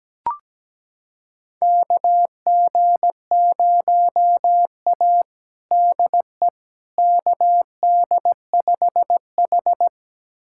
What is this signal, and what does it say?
Morse code, heard as a keyed tone, about 17 wpm, 700 Hz. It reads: KG0A DE KD5H
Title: USA Callsign DE Callsign - 4 Characters - 17wpm